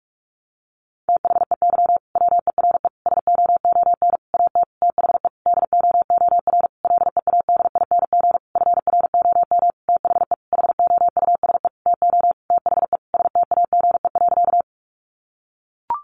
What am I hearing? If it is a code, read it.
Morse code, heard as a keyed tone, about 44 wpm, 700 Hz. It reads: THEY WERE SOON AT THE DOOR LEADING FROM THE HOUSE TO THE STAGE.
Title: Sentences from Top 1200 Words 45wpm